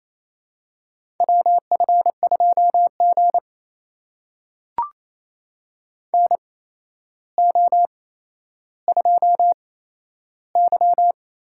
Morse code, heard as a keyed tone, about 28 wpm, spaced as for 12 wpm, 700 Hz. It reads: WF2G NO2Y
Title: USA Call Signs - 4 Characters - 8x Character Spacing - 28wpm